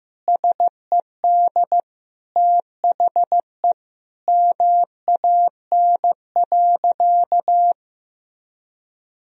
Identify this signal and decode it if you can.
Morse code, heard as a keyed tone, about 15 wpm, 700 Hz. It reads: SED THE MAN.
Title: Sentences from Top 400 Words and Abbreviations 15wpm